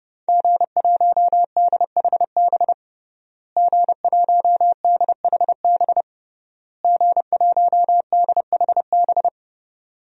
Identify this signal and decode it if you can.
Morse code, heard as a keyed tone, about 30 wpm, 700 Hz. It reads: G1DHB G1DHB G1DHB
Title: DXCC-100 Call Signs - 5 Characters - Repeated 3x - 30wpm